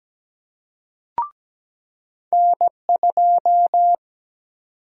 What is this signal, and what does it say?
Morse code, heard as a keyed tone, about 17 wpm, 700 Hz. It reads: N2
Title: USA Call Sign Prefixes - 17wpm